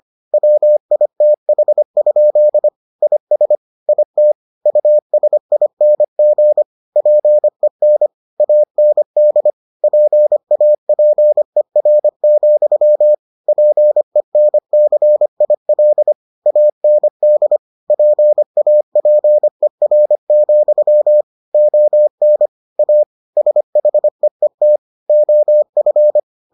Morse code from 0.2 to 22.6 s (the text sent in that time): WITH? IS IT USING PEN AND PAPER, PENCIL AND PAPER, ON